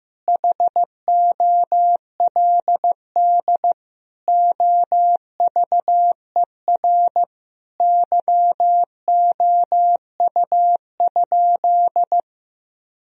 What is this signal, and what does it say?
Morse code, heard as a keyed tone, about 15 wpm, 700 Hz. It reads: HOLD OVER YOU?
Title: Sentences from Top 1000 Words - Review Part 2 - 15wpm